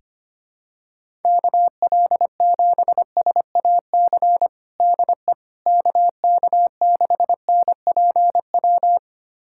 Morse code, heard as a keyed tone, about 25 wpm, 700 Hz. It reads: KL7SAC DE KK6NPW
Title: USA Callsign DE Callsign - 6 Characters - 25wpm